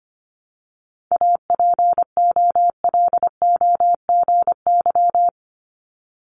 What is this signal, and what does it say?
Morse code, heard as a keyed tone, about 25 wpm, 700 Hz. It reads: APOLOGY
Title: Multi-letter Phonemes 25wpm